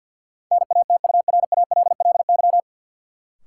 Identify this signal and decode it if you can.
Morse code, heard as a keyed tone, about 48 wpm, 700 Hz. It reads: NATURALLY
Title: Sentences from Top 1300 Words - New Word Drill Alt - 50wpm